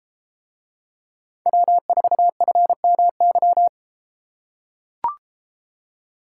Morse code, heard as a keyed tone, about 31 wpm, 700 Hz. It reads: W4FMY